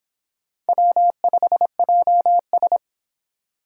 Morse code, heard as a keyed tone, about 29 wpm, 700 Hz. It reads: W5JS